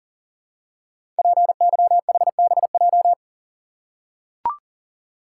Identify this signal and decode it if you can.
Morse code, heard as a keyed tone, about 40 wpm, 700 Hz. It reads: PYHBJ
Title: Code Groups - 5 Letters - 40wpm